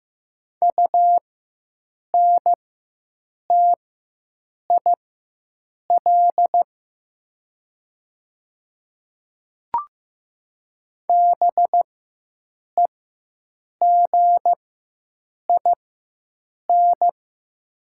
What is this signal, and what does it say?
Morse code, heard as a keyed tone, about 15 wpm, spaced as for 7 wpm, 700 Hz. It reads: UNTIL BEGIN
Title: 4x Character Spacing - Top 300 Words - 15wpm